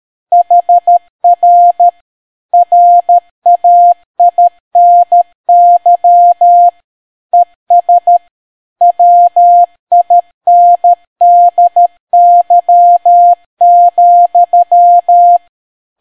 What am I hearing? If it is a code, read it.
Morse code, heard as a keyed tone, about 13 wpm, 700 Hz. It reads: HR RAINY ES WINDY,